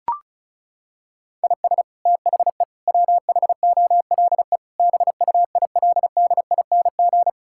Morse code, heard as a keyed tone, about 36 wpm, 700 Hz. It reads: IS THE WHOLE BUILDING
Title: Sentences from Top 1000 Words - Review Part 5 - 35wpm